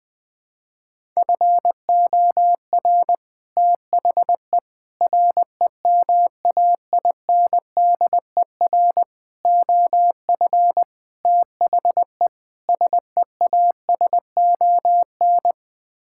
Morse code, heard as a keyed tone, about 20 wpm, 700 Hz. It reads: FOR THE REMAINDER OF THE SEASON
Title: Sets of 6 Words - 20wpm